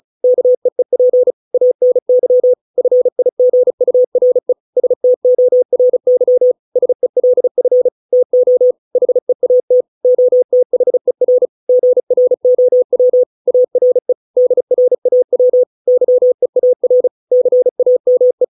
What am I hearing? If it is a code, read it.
Morse code, heard as a keyed tone, about 37 wpm, 500 Hz. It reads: KEEP ANY FIGURE STORY SELF TO HEAT OTHER GROW ARE DRAW YEAR CAME